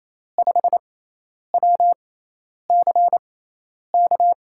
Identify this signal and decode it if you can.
Morse code, heard as a keyed tone, about 27 wpm, spaced as for 10 wpm, 700 Hz. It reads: 5WCK
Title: USA Call Signs - 5 Characters - 6x Character Spacing - 28wpm